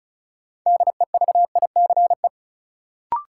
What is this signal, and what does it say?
Morse code, heard as a keyed tone, about 34 wpm, 700 Hz. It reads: DEVICE
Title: Sentences from Top 1500 Words - New Word Drill - 35wpm